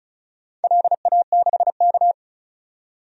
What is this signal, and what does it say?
Morse code, heard as a keyed tone, about 36 wpm, 700 Hz. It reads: LA6K
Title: DXCC-100 Call Signs - 4 Characters - 35wpm